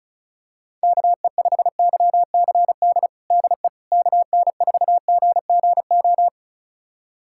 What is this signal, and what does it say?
Morse code, heard as a keyed tone, about 34 wpm, 700 Hz. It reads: KE5YCD DE KN4GGO
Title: USA Callsign DE Callsign - 6 Characters - 35wpm